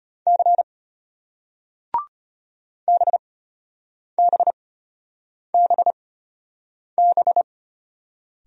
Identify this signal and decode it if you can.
Morse code, heard as a keyed tone, about 33 wpm, spaced as for 8 wpm, 700 Hz. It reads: C BBBB